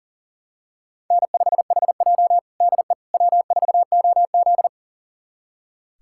Morse code, heard as a keyed tone, about 40 wpm, 700 Hz. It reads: N5HJ DE W4OZ
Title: USA Callsign DE Callsign - 4 Characters - 40wpm